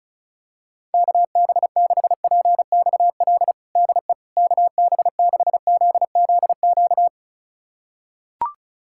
Morse code, heard as a keyed tone, about 36 wpm, 700 Hz. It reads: KB6PXL DE KB6ZZQ
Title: USA Callsign DE Callsign - 6 Characters - 35wpm